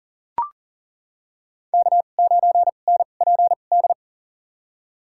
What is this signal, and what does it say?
Morse code, heard as a keyed tone, about 40 wpm, spaced as for 29 wpm, 700 Hz. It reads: K9NPD
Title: USA Call Signs - 2x Character Spacing - 40wpm